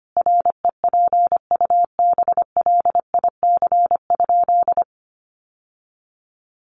Morse code, heard as a keyed tone, about 25 wpm, 700 Hz. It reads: REPUBLIC?
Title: Sentences from Top 1500 Words 25wpm